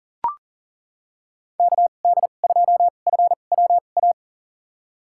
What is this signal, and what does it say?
Morse code, heard as a keyed tone, about 40 wpm, spaced as for 29 wpm, 700 Hz. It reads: KD2FWA
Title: USA Call Signs - 6 Characters - 2x Character Spacing - 40wpm